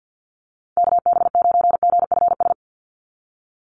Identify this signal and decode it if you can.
Morse code, heard as a keyed tone, about 50 wpm, 700 Hz. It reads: KB9GFS
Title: USA Call Signs 50wpm